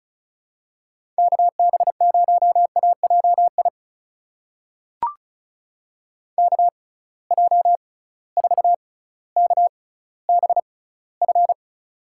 Morse code, heard as a keyed tone, about 36 wpm, spaced as for 18 wpm, 700 Hz. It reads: KB0AJI KJ4KBF
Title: USA Call Signs - 6 Characters - 6x Character Spacing - 35wpm